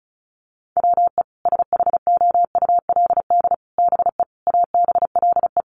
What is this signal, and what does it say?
Morse code, heard as a keyed tone, about 34 wpm, 700 Hz. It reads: WE SHOULD BE ABLE
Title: Sets of 5 Words Encore2 35wpm